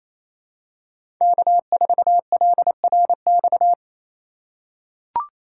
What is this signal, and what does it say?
Morse code, heard as a keyed tone, about 28 wpm, 700 Hz. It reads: K4LRX
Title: K1USN SST QSO Practice - 28wpm